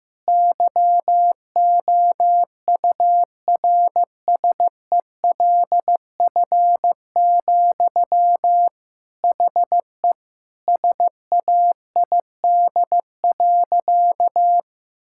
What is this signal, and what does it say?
Morse code, heard as a keyed tone, about 15 wpm, 700 Hz. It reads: YOURSELF, HE SAID.
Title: Sentences from Top 1600 Words 15wpm